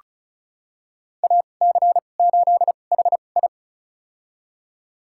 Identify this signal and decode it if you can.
Morse code, heard as a keyed tone, about 36 wpm, spaced as for 25 wpm, 700 Hz. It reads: AC8HI
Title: USA Call Signs - 5 Characters - 2x Character Spacing - 35wpm